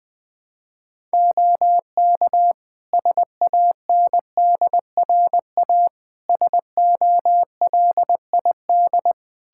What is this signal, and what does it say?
Morse code, heard as a keyed tone, about 20 wpm, 700 Hz. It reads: OK SANDRA SOLID
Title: Practice QSO - Exchange 3 - More Info - 20wpm